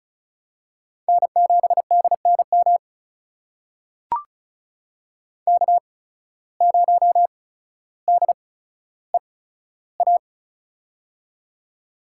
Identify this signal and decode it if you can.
Morse code, heard as a keyed tone, about 36 wpm, spaced as for 15 wpm, 700 Hz. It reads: N7DNM K0DEA
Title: USA Call Signs - 5 Characters - 8x Character Spacing - 35wpm